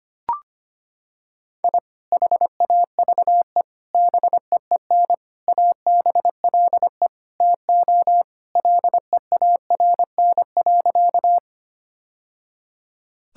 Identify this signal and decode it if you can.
Morse code, heard as a keyed tone, about 25 wpm, 700 Hz. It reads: I HAVE BEEN ABLE TO LEARN.
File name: practice / 25wpm / Sentences with 3-Word Phrases - 25wpm